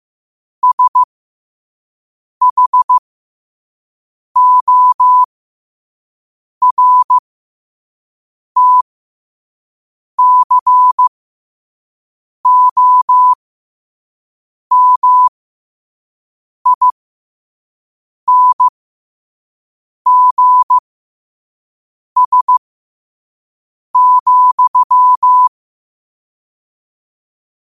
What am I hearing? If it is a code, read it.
Morse code, heard as a keyed tone, about 15 wpm, spaced as for 5 wpm, 1000 Hz. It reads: SHORTCOMINGS,
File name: qotdfiles / 5wpm